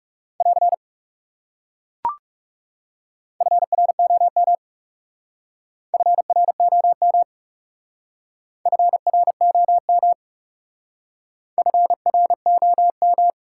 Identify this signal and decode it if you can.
Morse code, heard as a keyed tone, about 39 wpm, 700 Hz. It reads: <AR> FROM FROM FROM FROM